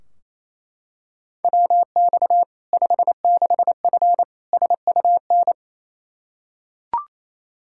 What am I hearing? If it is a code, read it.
Morse code, heard as a keyed tone, about 28 wpm, 700 Hz. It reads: WX 56F SUN
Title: Practice QSO - Exchange 2 - Info - 28wpm